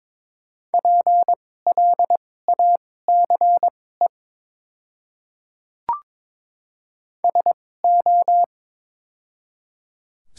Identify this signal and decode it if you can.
Morse code, heard as a keyed tone, about 22 wpm, spaced as for 16 wpm, 700 Hz. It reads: PLACE SO
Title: 2x Character Spacing - Top 300 Words - 22wpm